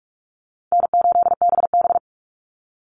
Morse code, heard as a keyed tone, about 44 wpm, 700 Hz. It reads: N8BB
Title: NA SOTA Chaser Callsigns 45wpm